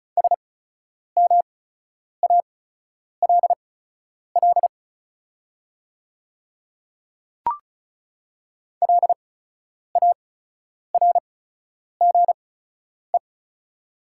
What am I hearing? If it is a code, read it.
Morse code, heard as a keyed tone, about 36 wpm, spaced as for 10 wpm, 700 Hz. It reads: SMALL LARGE
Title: 8x Character Spacing - Top 200 Words - 35wpm